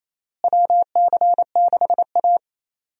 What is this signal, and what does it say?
Morse code, heard as a keyed tone, about 28 wpm, 700 Hz. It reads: WC6A
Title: USA Call Signs - 4 Characters - Repeated 3x - 28wpm